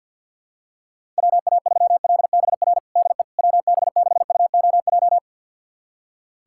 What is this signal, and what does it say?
Morse code, heard as a keyed tone, about 47 wpm, 700 Hz. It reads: WA3LBR DE WB6UOJ
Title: USA Callsign DE Callsign - 6 Characters - 50wpm